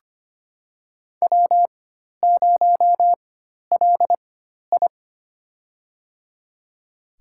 Morse code, heard as a keyed tone, about 25 wpm, spaced as for 12 wpm, 700 Hz. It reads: W0LI
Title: USA Call Signs - 4x Character Spacing - 25wpm